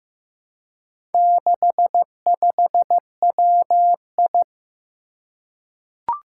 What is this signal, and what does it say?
Morse code, heard as a keyed tone, about 15 wpm, 700 Hz. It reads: 65WI
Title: Four Challenging Letter-Number Combinations 15wpm